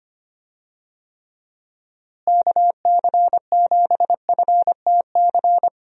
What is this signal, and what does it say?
Morse code, heard as a keyed tone, about 25 wpm, 700 Hz. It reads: KC7FTC
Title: USA Call Signs - 6 Characters - Repeated 3x - 25wpm